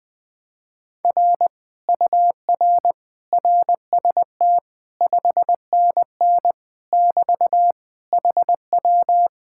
Morse code, heard as a keyed tone, about 20 wpm, 700 Hz. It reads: R UR RST 5NN <BT> HW
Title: Speed-Racing Common QSO Phrases Encore - 17wpm